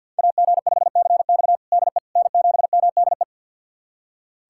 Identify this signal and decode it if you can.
Morse code, heard as a keyed tone, about 48 wpm, 700 Hz. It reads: AG5CX DE N7MDE
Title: USA Callsign DE Callsign - 5 Characters - 50wpm